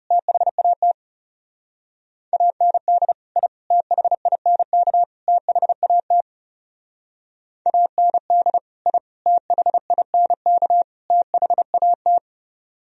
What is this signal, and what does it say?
Morse code, heard as a keyed tone, about 37 wpm, 700 Hz. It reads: THAT AND I THINK THAT AND I THINK THAT
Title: Speed-Racing - Sets of 4 Words 30wpm